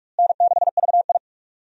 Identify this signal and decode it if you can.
Morse code, heard as a keyed tone, about 44 wpm, 700 Hz. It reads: N6VI
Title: USA Call Signs - 4 Characters - Repeated 2x - 45wpm